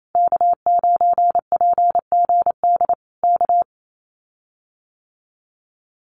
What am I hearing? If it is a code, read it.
Morse code, heard as a keyed tone, about 28 wpm, 700 Hz. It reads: K9PGD K
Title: Common QSO Phrases Encore 28wpm